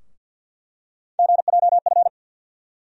Morse code, heard as a keyed tone, about 48 wpm, 700 Hz. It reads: GJF
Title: USA Call Sign Postfixes - 3 Letters 50wpm